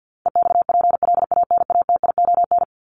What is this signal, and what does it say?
Morse code, heard as a keyed tone, about 50 wpm, 700 Hz. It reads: EXPLANATION
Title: Top 1751-2000 Words 50wpm